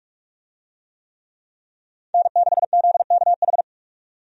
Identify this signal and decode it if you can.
Morse code, heard as a keyed tone, about 46 wpm, 700 Hz. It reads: N6ZKH
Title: USA Call Signs - 5 Characters - 45wpm